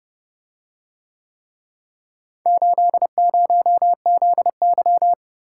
Morse code, heard as a keyed tone, about 29 wpm, 700 Hz. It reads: 80ZY